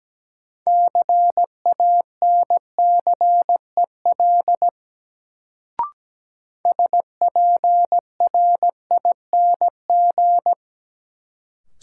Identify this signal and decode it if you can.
Morse code, heard as a keyed tone, about 17 wpm, 700 Hz. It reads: CANCEL SPRING
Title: Sentences from Top 1200 Words - New Word Drill Alt - 17wpm